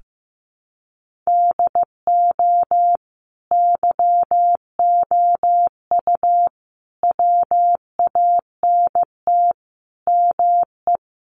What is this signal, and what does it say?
Morse code, heard as a keyed tone, about 15 wpm, 700 Hz. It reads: DO YOU WANT ME
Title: Sets of 5 Words 15wpm